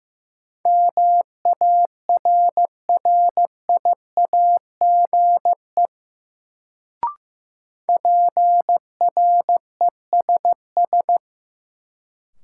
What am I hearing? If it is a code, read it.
Morse code, heard as a keyed tone, about 15 wpm, 700 Hz. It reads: MARRIAGE PRESS
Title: Sentences from Top 800 Words - New Word Drill - 15wpm